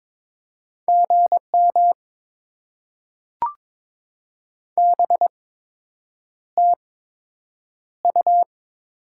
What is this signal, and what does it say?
Morse code, heard as a keyed tone, about 22 wpm, spaced as for 8 wpm, 700 Hz. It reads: GM BTU
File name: Top 20 QSO Elements - 8x Character Spacing - 22wpm